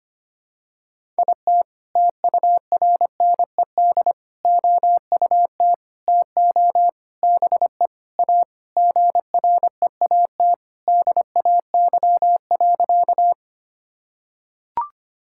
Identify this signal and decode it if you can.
Morse code, heard as a keyed tone, about 25 wpm, 700 Hz. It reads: IT TURNED OUT TO BE A GREAT DAY.
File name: practice / 25wpm / Sentences with 4-Word Phrases - 25wpm